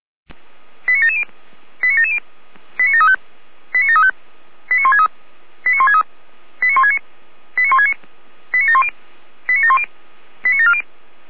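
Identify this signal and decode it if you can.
Five-tone selective call (ZVEI), heard as quick runs of five tones, tones of about 80 ms. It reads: 89770 89770 89735 89735 89163 89163 89169 89169 89710 89710 89740
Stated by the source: ZVEI five-tone call received by SDR.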